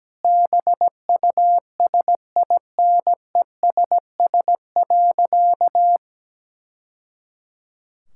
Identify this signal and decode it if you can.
Morse code, heard as a keyed tone, about 17 wpm, 700 Hz. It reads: BUSINESS.
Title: Sentences from Top 1000 Words - Review Part 6 - 17wpm